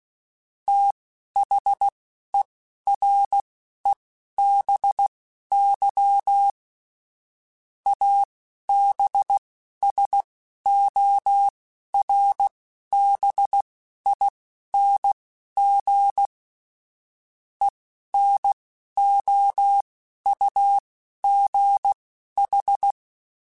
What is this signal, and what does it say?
Morse code, heard as a keyed tone, about 16 wpm, spaced as for 11 wpm, 800 Hz. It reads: THEREBY ABSORBING ENOUGH
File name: Technical Subjects / Technical Subjects-mp3s / XC @10